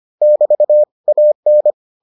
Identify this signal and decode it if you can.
Morse code, heard as a keyed tone, about 25 wpm, 600 Hz. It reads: <BT> AN